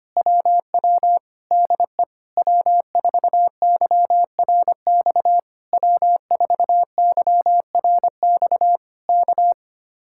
Morse code, heard as a keyed tone, about 24 wpm, 700 Hz. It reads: WW DE W4YRX W4YRX K
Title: Speed-Racing Common QSO Phrases Encore - 25wpm